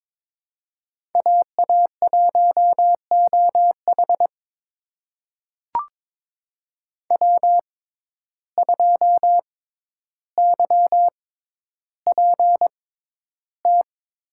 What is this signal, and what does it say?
Morse code, heard as a keyed tone, about 22 wpm, spaced as for 11 wpm, 700 Hz. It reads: AA1OH W2YPT
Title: USA Call Signs - 5 Characters - 6x Character Spacing - 22wpm